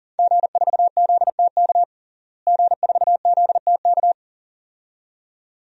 Morse code, heard as a keyed tone, about 40 wpm, 700 Hz. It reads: G4ZTK G4ZTK
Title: DXCC-100 Call Signs - 5 Characters - Repeated 2x - 40wpm